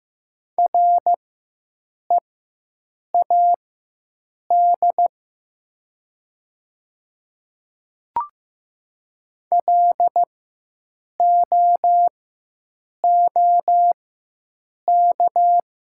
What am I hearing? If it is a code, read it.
Morse code, heard as a keyed tone, about 15 wpm, spaced as for 7 wpm, 700 Hz. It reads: READ LOOK
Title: 4x Character Spacing - Top 200 Words - 15wpm